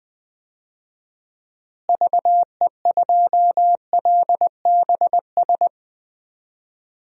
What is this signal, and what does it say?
Morse code, heard as a keyed tone, about 20 wpm, 700 Hz. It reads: VE2LBS
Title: DXCC-100 Call Signs - 6 Characters - Repeated 2x - 20wpm